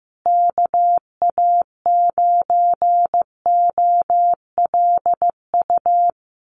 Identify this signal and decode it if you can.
Morse code, heard as a keyed tone, about 15 wpm, 700 Hz. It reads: KA9OLU
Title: USA Call Signs Repeated 2x 15wpm